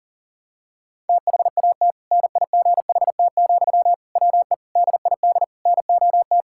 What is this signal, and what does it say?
Morse code, heard as a keyed tone, about 41 wpm, 700 Hz. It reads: THAT NIGHT, WE DID NOT
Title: Sentences from Top 1000 Words - Review Part 6 - 40wpm